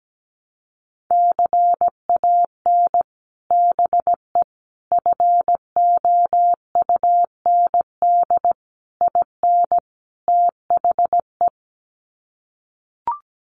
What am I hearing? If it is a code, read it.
Morse code, heard as a keyed tone, about 17 wpm, 700 Hz. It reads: CAN BE FOUND IN THE
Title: Sets of 5 Words Encore 17wpm